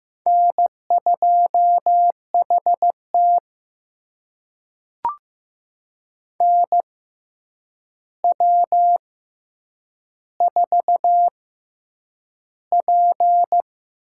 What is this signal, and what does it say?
Morse code, heard as a keyed tone, about 15 wpm, spaced as for 8 wpm, 700 Hz. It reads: N2HT NW4P
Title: USA Call Signs - 4 Characters - 6x Character Spacing - 15wpm